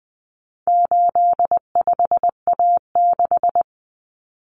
Morse code, heard as a keyed tone, about 20 wpm, 700 Hz. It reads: 85A6
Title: Common Cut Numbers - 4-Digit 20wpm